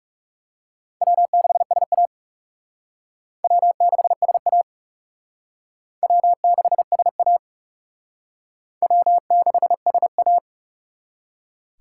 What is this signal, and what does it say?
Morse code, heard as a keyed tone, about 37 wpm, 700 Hz. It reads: W6SA W6SA W6SA W6SA
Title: Speed Racing - USA 1x2 Call Signs 30wpm